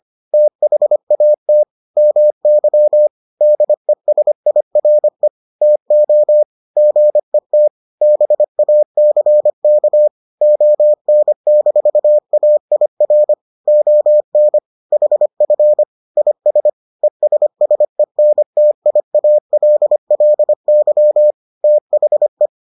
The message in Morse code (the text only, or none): THAT MY DESIRE TO GET BACK ON-AIR ON HF IS ESSENTIALLY THE